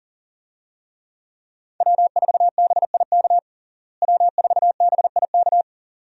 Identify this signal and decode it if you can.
Morse code, heard as a keyed tone, about 40 wpm, 700 Hz. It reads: W4BIK W4BIK
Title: USA Call Signs - 5 Characters - Repeated 2x - 40wpm